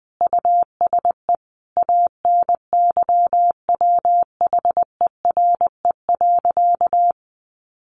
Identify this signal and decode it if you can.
Morse code, heard as a keyed tone, about 20 wpm, 700 Hz. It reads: USE ANYWHERE.
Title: Sentences from Top 1400 Words 20wpm